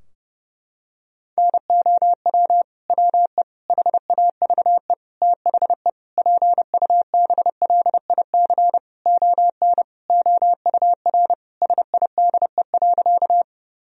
Morse code, heard as a keyed tone, about 32 wpm, 700 Hz. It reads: NOW WE HAVE THE PUBLIC ON OUR SIDE.
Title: Sentences from Top 1000 Words - Review Part 3 - 30wpm